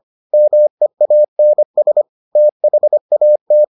Morse code, heard as a keyed tone, about 25 wpm, 600 Hz. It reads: MEANS THAT